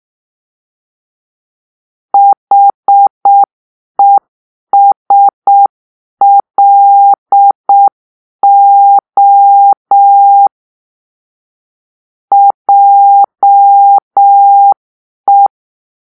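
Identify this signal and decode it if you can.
Morse code, heard as a keyed tone, about 6 wpm, 800 Hz. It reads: HESLO JE
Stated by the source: Morse-code cipher made by a player team for the Kachna puzzle game.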